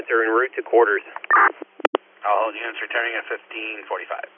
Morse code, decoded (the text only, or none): none